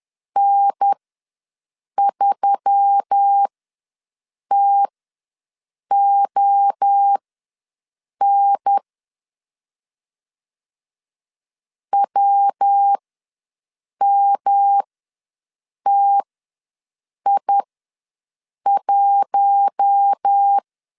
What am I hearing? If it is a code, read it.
Morse code, heard as a keyed tone, about 10 wpm, spaced as for 6 wpm, 800 Hz. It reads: N3TON WMTI1
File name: LEZIONI / 5 LBVC / 5 L ws 10wpm 3636